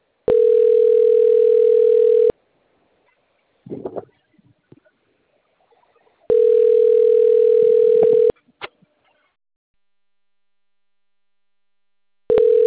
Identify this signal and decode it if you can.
Ringback tone (North America), 440 + 480 Hz.